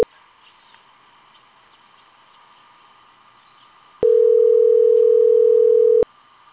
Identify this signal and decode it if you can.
Ringback tone (North America), 440 + 480 Hz.